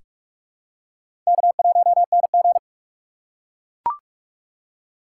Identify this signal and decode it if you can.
Morse code, heard as a keyed tone, about 44 wpm, 700 Hz. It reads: K1NG
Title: USA Call Signs - 4 Characters - 4x Character Spacing - 45wpm